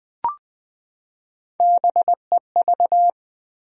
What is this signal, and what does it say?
Morse code, heard as a keyed tone, about 21 wpm, 700 Hz. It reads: BEV
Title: USA Common Names - Max 5 Letters - 20wpm